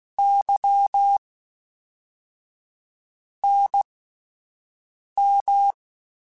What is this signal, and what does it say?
Morse code, heard as a keyed tone, about 16 wpm, spaced as for 4 wpm, 800 Hz. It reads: YNM
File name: CW Comparisons / CW Comparisons-mp3s / CW Comparisons Broken Characters @10